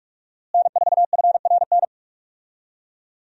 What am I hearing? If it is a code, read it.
Morse code, heard as a keyed tone, about 46 wpm, 700 Hz. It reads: N4FRN
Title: USA Call Signs - 5 Characters - Repeated 3x - 45wpm